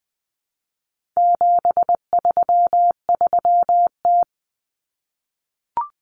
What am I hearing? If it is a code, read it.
Morse code, heard as a keyed tone, about 20 wpm, 700 Hz. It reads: 733T
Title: Common Cut Numbers - 4-Digit 20wpm